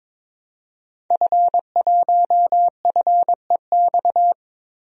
Morse code, heard as a keyed tone, about 22 wpm, 700 Hz. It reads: F1FEX
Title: DXCC-100 Call Signs - 5 Characters - Repeated 2x - 22wpm